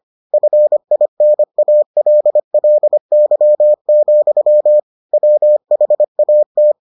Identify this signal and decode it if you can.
Morse code, heard as a keyed tone, about 25 wpm, 600 Hz. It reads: FINALLY, WHAT